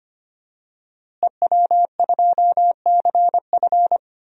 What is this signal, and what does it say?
Morse code, heard as a keyed tone, about 25 wpm, 700 Hz. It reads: EW2CF
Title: DXCC-100 Call Signs - 5 Characters - Repeated 3x - 25wpm